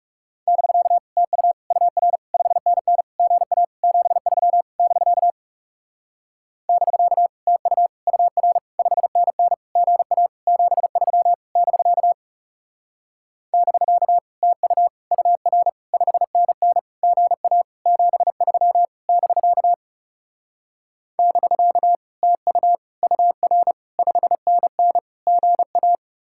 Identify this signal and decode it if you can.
Morse code, heard as a keyed tone, about 37 wpm, 700 Hz. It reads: <BK> TU UR 5NN GA 73 <BK> <BK> TU UR 5NN GA 73 <BK> <BK> TU UR 5NN GA 73 <BK> <BK> TU UR 5NN GA
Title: Speed-Racing Common QSO Phrases - 30wpm